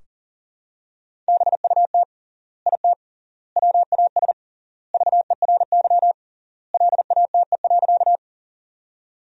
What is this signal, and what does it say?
Morse code, heard as a keyed tone, about 37 wpm, 700 Hz. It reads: BUT IT WAS VERY LATE.
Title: 3x Word-Spacing - Sentences from Top 300 Words - 40wpm